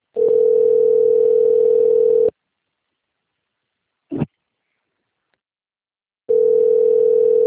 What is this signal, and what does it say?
Ringback tone (North America), 440 + 480 Hz.